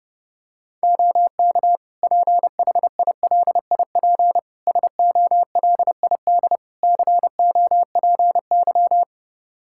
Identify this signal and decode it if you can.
Morse code, heard as a keyed tone, about 32 wpm, 700 Hz. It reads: OK PHILIP SOLID COPY
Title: K1USN SST QSO Practice 30wpm